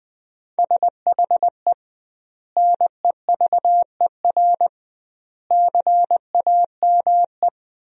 Morse code, heard as a keyed tone, about 20 wpm, 700 Hz. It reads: SHE NEVER CAME
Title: 2x Word-Spacing - Sentences from Top 300 Words - 20wpm